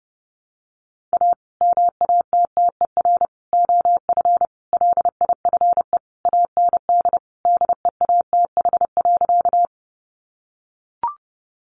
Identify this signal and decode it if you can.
Morse code, heard as a keyed tone, about 30 wpm, 700 Hz. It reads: A MATTER OF LIFE AND DEATH.
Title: Sentences from Top 1100 Words 30wpm